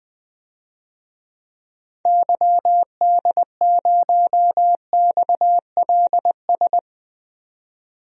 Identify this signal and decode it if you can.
Morse code, heard as a keyed tone, about 21 wpm, 700 Hz. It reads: YD0XLS
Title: DXCC-100 Call Signs - 6 Characters - 20wpm